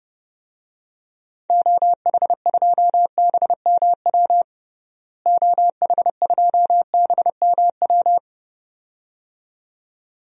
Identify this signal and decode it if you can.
Morse code, heard as a keyed tone, about 30 wpm, 700 Hz. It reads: OH2BMW OH2BMW
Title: DXCC-100 Call Signs - 6 Characters - Repeated 2x - 30wpm